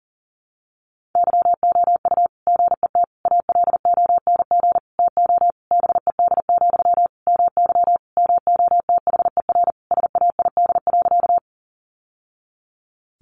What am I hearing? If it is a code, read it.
Morse code, heard as a keyed tone, about 40 wpm, 700 Hz. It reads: YOU GET ALONG TO BED, MY MOTHER SAID.
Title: Sentences from Top 700 Words 40wpm